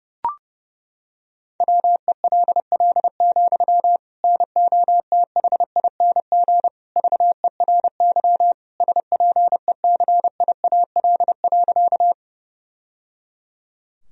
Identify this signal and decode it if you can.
Morse code, heard as a keyed tone, about 32 wpm, 700 Hz. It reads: WELL, NOTHING VERY SPECIAL.
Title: Sentences from Top 1000 Words - Review Part 3 - 30wpm